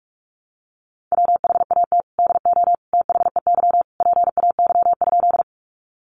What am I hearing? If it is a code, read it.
Morse code, heard as a keyed tone, about 44 wpm, 700 Hz. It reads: WHAT DO THEY PAY?
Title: Sentences from Top 700 Words 45wpm